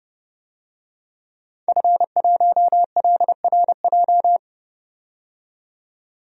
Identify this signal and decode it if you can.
Morse code, heard as a keyed tone, about 30 wpm, 700 Hz. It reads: F1LRJ
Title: DXCC-100 Call Signs - 5 Characters - 30wpm